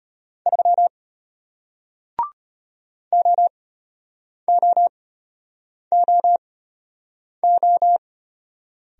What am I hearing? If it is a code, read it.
Morse code, heard as a keyed tone, about 34 wpm, spaced as for 8 wpm, 700 Hz. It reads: 3 OOOO